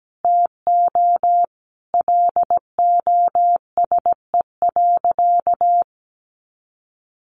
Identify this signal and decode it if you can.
Morse code, heard as a keyed tone, about 17 wpm, 700 Hz. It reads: TO LOSE.
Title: Sentences from Top 1100 Words 17wpm